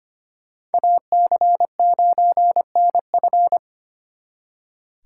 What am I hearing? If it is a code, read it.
Morse code, heard as a keyed tone, about 32 wpm, 700 Hz. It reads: AC9NF